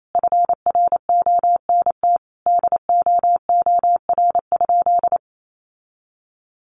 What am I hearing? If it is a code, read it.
Morse code, heard as a keyed tone, about 28 wpm, 700 Hz. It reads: FRONT DOOR?
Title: Sentences from Top 1300 Words 28wpm